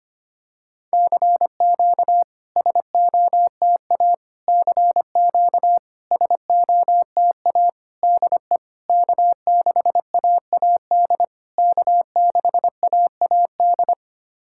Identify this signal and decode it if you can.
Morse code, heard as a keyed tone, about 25 wpm, 690 Hz. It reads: CQ SOTA CQ SOTA DE K6AAD K6AAD
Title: SOTA QSO Practice 25wpm